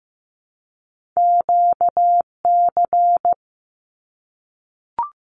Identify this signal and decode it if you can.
Morse code, heard as a keyed tone, about 15 wpm, 700 Hz. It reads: QC
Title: Canadian Provinces and Territories 15wpm